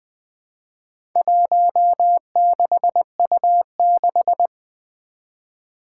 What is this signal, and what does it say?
Morse code, heard as a keyed tone, about 20 wpm, 700 Hz. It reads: 16U6
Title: Four Challenging Letter-Number Combinations 20wpm